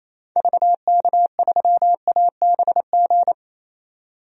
Morse code, heard as a keyed tone, about 28 wpm, 700 Hz. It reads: VK3ABG
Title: DXCC-100 Call Signs - 6 Characters - 28wpm